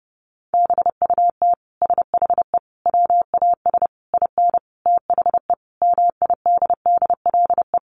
Morse code, heard as a keyed tone, about 30 wpm, 700 Hz. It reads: BUT SHE WAS IN THE MIDDLE
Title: Sentences from Top 1200 Words 30wpm